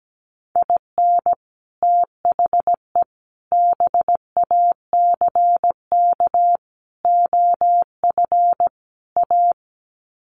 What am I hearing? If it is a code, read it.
Morse code, heard as a keyed tone, about 17 wpm, 700 Hz. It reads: IN THE BACK OF A
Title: Sets of 5 Words Encore2 17wpm